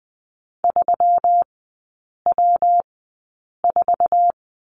Morse code, heard as a keyed tone, about 20 wpm, spaced as for 8 wpm, 690 Hz. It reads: 3W4
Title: 159 Lesson 26e - Letter g - Three Characters 20wpm